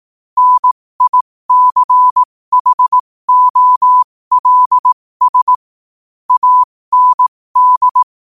Morse code, heard as a keyed tone, about 18 wpm, spaced as for 15 wpm, 1000 Hz. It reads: NICHOLS AND
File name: qotdfiles / 15wpm